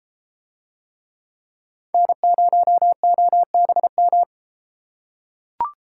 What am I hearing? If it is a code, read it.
Morse code, heard as a keyed tone, about 33 wpm, 700 Hz. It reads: N0OBM